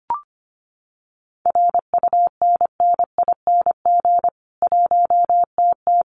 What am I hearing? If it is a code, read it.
Morse code, heard as a keyed tone, about 25 wpm, 700 Hz. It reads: RUNNING 1TT
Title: Common QSO Phrases 25wpm